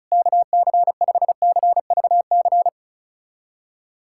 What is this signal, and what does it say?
Morse code, heard as a keyed tone, about 36 wpm, 700 Hz. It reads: KC5CVC
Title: USA Callsign DE Callsign 35wpm